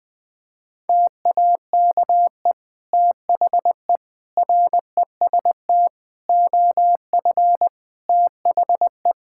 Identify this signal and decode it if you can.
Morse code, heard as a keyed tone, about 20 wpm, 700 Hz. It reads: TAKE THE REST OF THE
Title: Sentences with 2-Word Phrases - 20wpm